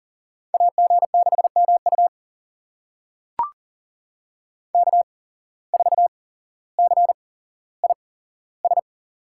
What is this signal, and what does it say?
Morse code, heard as a keyed tone, about 40 wpm, spaced as for 17 wpm, 700 Hz. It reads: AG6MU K4CIS
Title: USA Call Signs - 5 Characters - 8x Character Spacing - 40wpm